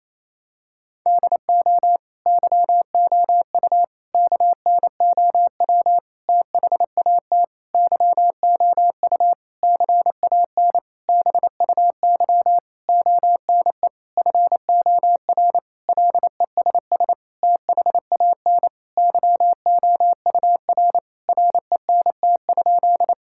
Morse code, heard as a keyed tone, about 28 wpm, 700 Hz. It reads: DO YOU KNOW THAT YOU CAN BUY ONE FOR LESS THAN YOUR RENT?
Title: Sentences from Top 1000 Words - Review Part 5 - 28wpm